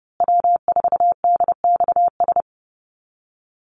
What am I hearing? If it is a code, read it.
Morse code, heard as a keyed tone, about 30 wpm, 700 Hz. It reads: W4DXS
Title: USA Call Signs Repeated 2x 30wpm